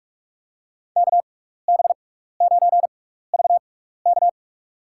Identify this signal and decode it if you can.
Morse code, heard as a keyed tone, about 46 wpm, spaced as for 15 wpm, 700 Hz. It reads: KB9VK
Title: USA Call Signs - 5 Characters - 6x Character Spacing - 45wpm